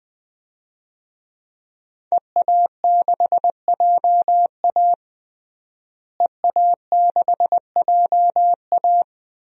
Morse code, heard as a keyed tone, about 20 wpm, 700 Hz. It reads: EA6JA EA6JA
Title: DXCC-100 Call Signs - 5 Characters - Repeated 2x - 20wpm